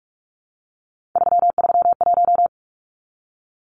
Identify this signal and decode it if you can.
Morse code, heard as a keyed tone, about 44 wpm, 700 Hz. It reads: 331
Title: Numbers - 3-Digit 45wpm